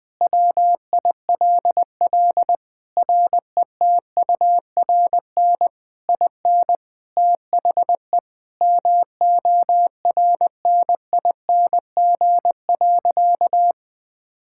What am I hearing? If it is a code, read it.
Morse code, heard as a keyed tone, about 20 wpm, 700 Hz. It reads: WILL RETURN IN THE MORNING.
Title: Sentences from Top 1000 Words - Review Part 6 - 20wpm